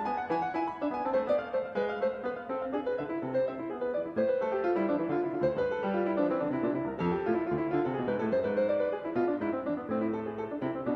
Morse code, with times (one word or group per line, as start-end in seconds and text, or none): none